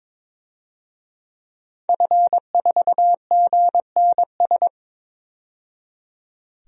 Morse code, heard as a keyed tone, about 21 wpm, 700 Hz. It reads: F4GNS
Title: DXCC-100 Call Signs - 5 Characters - 22wpm